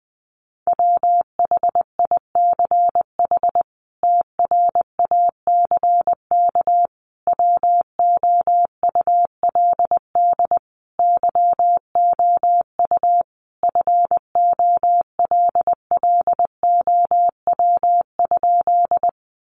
Morse code, heard as a keyed tone, about 20 wpm, 700 Hz. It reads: WHICH TRACK WOULD YOU FOLLOW?
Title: Sentences from Top 1100 Words 20wpm